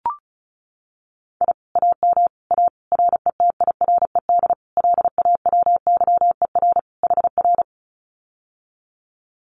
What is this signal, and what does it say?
Morse code, heard as a keyed tone, about 36 wpm, 700 Hz. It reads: I AM A RETIRED LAWYER HR
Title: Common QSO Phrases Encore 35wpm